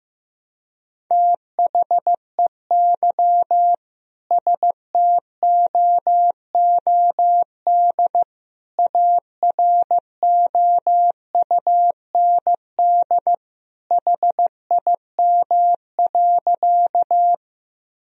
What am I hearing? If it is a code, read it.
Morse code, heard as a keyed tone, about 15 wpm, 700 Hz. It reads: THEY STOOD AROUND HIM.
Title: Sentences from Top 1000 Words - Review Part 3 - 15wpm